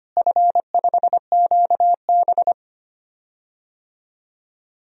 Morse code, heard as a keyed tone, about 25 wpm, 700 Hz. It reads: F5QB
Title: DXCC-100 Call Signs - 4 Characters - Repeated 3x - 25wpm